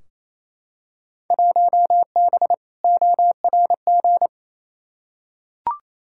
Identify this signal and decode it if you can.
Morse code, heard as a keyed tone, about 28 wpm, 700 Hz. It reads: 1B ORG
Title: Field Day Exchanges Encore - 28wpm